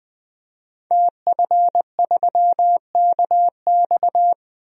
Morse code, heard as a keyed tone, about 20 wpm, 700 Hz. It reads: TF3KX
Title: DXCC-100 Call Signs - 5 Characters - Repeated 2x - 20wpm